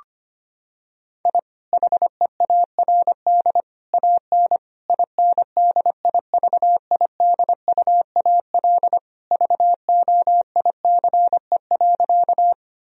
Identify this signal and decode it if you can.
Morse code, heard as a keyed tone, about 25 wpm, 700 Hz. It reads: I HEARD AN INDIVIDUAL VOICE.
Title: Sentences from Top 1000 Words - Review Part 4 - 25wpm